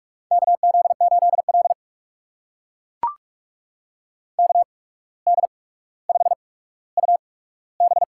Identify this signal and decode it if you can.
Morse code, heard as a keyed tone, about 44 wpm, spaced as for 18 wpm, 690 Hz. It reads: KZ8L KD5UB
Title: USA Call Signs - 8x Character Spacing - 45wpm